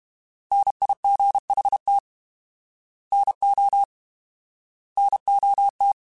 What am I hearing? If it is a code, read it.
Morse code, heard as a keyed tone, about 32 wpm, 800 Hz. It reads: NIGHT NO NOT
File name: Alphabetical / Alphabetical mp3s / Ns @30